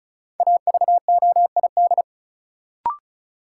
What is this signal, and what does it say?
Morse code, heard as a keyed tone, about 36 wpm, 700 Hz. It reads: AVOID
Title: Sentences from Top 1000 Words - Review Part 6 - New Word Drill - 35wpm